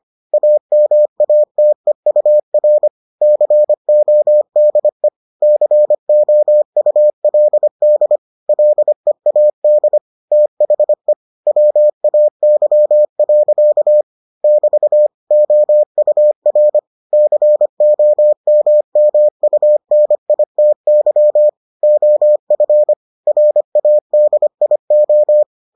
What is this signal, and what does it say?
Morse code, heard as a keyed tone, about 25 wpm, 600 Hz. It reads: AMATEUR CODE COULD LEAD THE WAY. <BT> OUR COMMUNITY OF RADIO